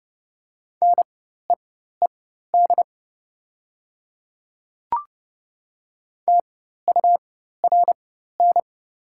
Morse code, heard as a keyed tone, about 30 wpm, spaced as for 14 wpm, 700 Hz. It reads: NEED TURN
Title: 4x Character Spacing - Top 300 Words - 30wpm